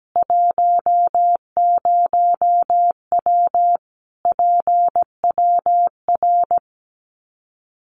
Morse code, heard as a keyed tone, about 17 wpm, 700 Hz. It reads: 10W PWR
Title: Common QSO Phrases 17wpm